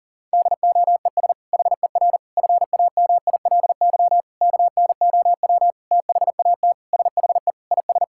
Morse code, heard as a keyed tone, about 40 wpm, 700 Hz. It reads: DOES HER FAMILY KNOW THAT SHE IS
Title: Sentences from Top 1000 Words - Review Part 6 - 40wpm